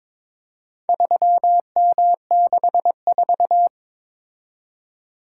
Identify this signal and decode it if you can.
Morse code, heard as a keyed tone, about 22 wpm, 700 Hz. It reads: 3M64
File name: practice / 22wpm / Four Letter-Number Combinations - 22wpm